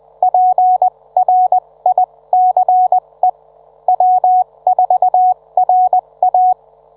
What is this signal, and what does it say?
Morse code, heard as a keyed tone, about 20 wpm, spaced as for 17 wpm, 740 Hz. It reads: PRICE W4RA